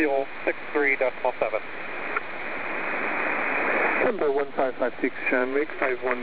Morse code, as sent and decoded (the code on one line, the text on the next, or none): none